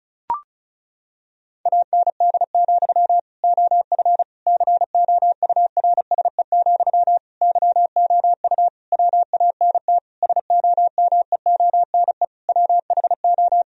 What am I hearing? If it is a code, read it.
Morse code, heard as a keyed tone, about 36 wpm, 700 Hz. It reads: AND, OF COURSE, YOU WANT SOMEONE WHO
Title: Sentences from Top 1000 Words - Review Part 1 - 35wpm